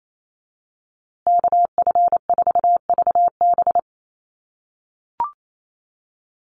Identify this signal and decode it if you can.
Morse code, heard as a keyed tone, about 28 wpm, 700 Hz. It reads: KF4VB
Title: USA Call Signs Repeated 2x 28wpm